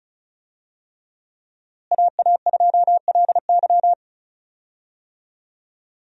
Morse code, heard as a keyed tone, about 36 wpm, 700 Hz. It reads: AA2LY
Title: USA Call Signs - 5 Characters - Repeated 3x - 35wpm